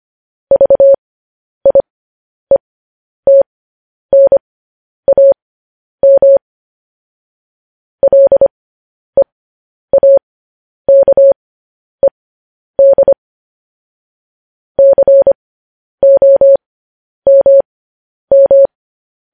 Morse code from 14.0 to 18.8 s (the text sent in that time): COMM